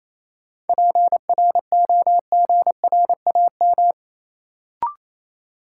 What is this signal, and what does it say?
Morse code, heard as a keyed tone, about 27 wpm, 700 Hz. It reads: PROGRAM
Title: Sentences from Top 1500 Words - New Word Drill Alt - 28wpm